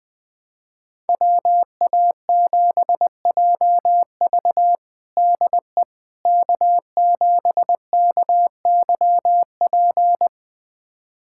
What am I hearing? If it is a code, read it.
Morse code, heard as a keyed tone, about 19 wpm, 700 Hz. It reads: WA7JV DE K7KYP
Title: USA Callsign DE Callsign - 5 Characters - 20wpm